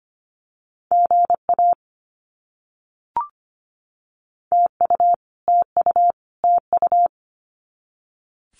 Morse code, heard as a keyed tone, about 25 wpm, 700 Hz. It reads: GA TU TU TU
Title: Abbreviations Repeated 3x - 25wpm